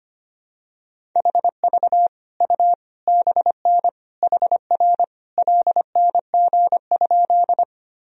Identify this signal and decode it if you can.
Morse code, heard as a keyed tone, about 25 wpm, 700 Hz. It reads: HV U BN HR LNG?
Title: Sentences from Top 200 Words and Abbreviations 25wpm